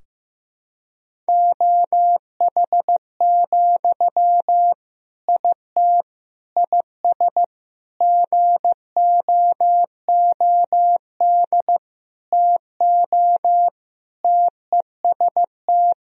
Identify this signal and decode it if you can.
Morse code, heard as a keyed tone, about 16 wpm, 700 Hz. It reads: OH, IT IS GOOD TO TEST
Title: Sentences from Top 1000 Words - Review Part 3 - 15wpm